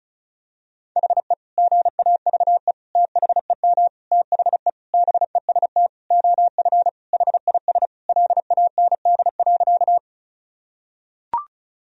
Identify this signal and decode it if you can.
Morse code, heard as a keyed tone, about 36 wpm, 700 Hz. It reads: HE GAVE THEM THE BEST OF HIS LAND.
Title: Sentences from Top 1000 Words - Review Part 1 - 35wpm